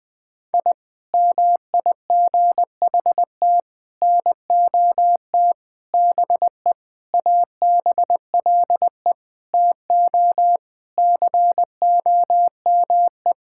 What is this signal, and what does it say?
Morse code, heard as a keyed tone, about 19 wpm, 700 Hz. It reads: I MIGHT NOT BE ABLE TO COME
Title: Sentences with 4-Word Phrases - 20wpm